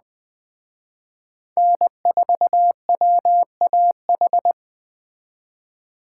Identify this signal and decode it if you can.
Morse code, heard as a keyed tone, about 20 wpm, 700 Hz. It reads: N4WAH